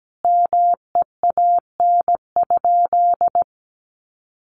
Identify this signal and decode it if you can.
Morse code, heard as a keyed tone, about 17 wpm, 700 Hz. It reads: MEAN?
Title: Sentences from Top 200 Words 17wpm